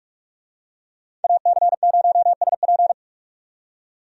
Morse code, heard as a keyed tone, about 42 wpm, 700 Hz. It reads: AC0SP
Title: USA Call Signs - 5 Characters - 45wpm